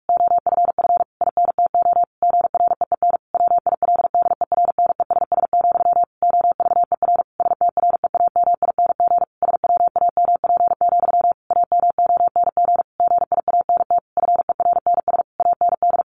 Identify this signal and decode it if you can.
Morse code, heard as a keyed tone, about 44 wpm, 700 Hz. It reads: OFF INTO GREEN WILDERNESS, OVER STREAMING SWAMP, AMONG GIANT FERNS AND
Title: A Sound Of Thunder 45wpm